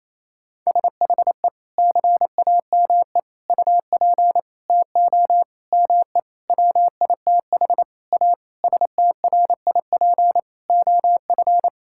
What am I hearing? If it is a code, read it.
Morse code, heard as a keyed tone, about 28 wpm, 700 Hz. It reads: SHE CAME UP TO ME WITH A STRIP OF
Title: Sentences from Top 1600 Words 28wpm